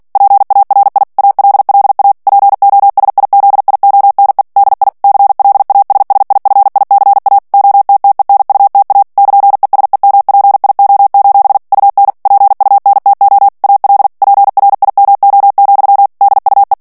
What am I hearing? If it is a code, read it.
Morse code, heard as a keyed tone, about 48 wpm, 800 Hz. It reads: PARI ALLA POSIZIONE DI CLASSIFICA OTTENUTA <KN>ESEMPIO: UN PUNTO AL PRIMO, DUE
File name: ITALIEN texte clair 2 vitesse 45 mots minute